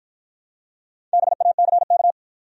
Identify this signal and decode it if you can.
Morse code, heard as a keyed tone, about 51 wpm, 700 Hz. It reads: BACK